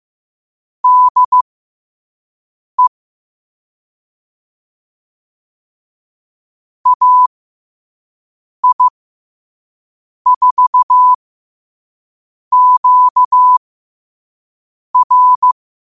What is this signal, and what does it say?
Morse code, heard as a keyed tone, about 15 wpm, spaced as for 5 wpm, 1000 Hz. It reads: DE AI4QR
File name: qotdfiles / 5wpm